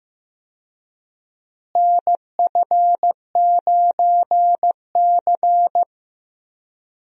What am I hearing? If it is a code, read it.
Morse code, heard as a keyed tone, about 15 wpm, 700 Hz. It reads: NF9C
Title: USA Call Signs - 4 Characters - Repeated 3x - 15wpm